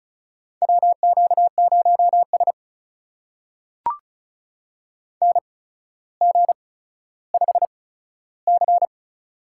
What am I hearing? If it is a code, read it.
Morse code, heard as a keyed tone, about 36 wpm, spaced as for 15 wpm, 700 Hz. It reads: WQ0S NG5C
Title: USA Call Signs - 4 Characters - 8x Character Spacing - 35wpm